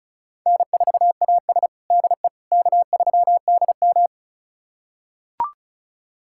Morse code, heard as a keyed tone, about 36 wpm, 700 Hz. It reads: N4AS DE K3DM
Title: USA Callsign DE Callsign - 4 Characters - 35wpm